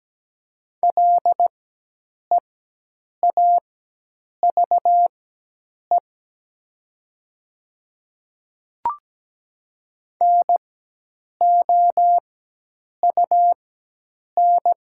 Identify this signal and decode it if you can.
Morse code, heard as a keyed tone, about 17 wpm, spaced as for 8 wpm, 700 Hz. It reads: LEAVE NOUN
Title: 4x Character Spacing - Top 500 Words - 17wpm